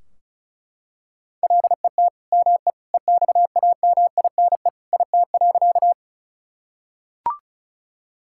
Morse code, heard as a keyed tone, about 36 wpm, 700 Hz. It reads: LET ME EXAMINE IT.
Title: Sentences from Top 1600 Words 35wpm